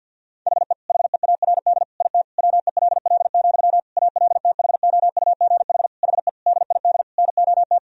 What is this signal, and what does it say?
Morse code, heard as a keyed tone, about 50 wpm, 700 Hz. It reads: HE HEARD IT WELL, ALTHOUGH HE DID NOT
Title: Sentences from Top 1000 Words - Review Part 3 - 50wpm